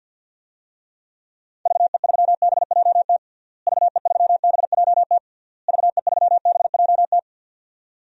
Morse code, heard as a keyed tone, about 52 wpm, 700 Hz. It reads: VE3BJT VE3BJT VE3BJT
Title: DXCC-100 Call Signs - 6 Characters - Repeated 3x - 50wpm